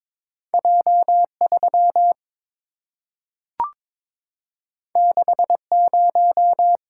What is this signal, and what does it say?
Morse code, heard as a keyed tone, about 22 wpm, 700 Hz. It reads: J3 60
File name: practice / 22wpm / Two Letter-Number Combinations - 22wpm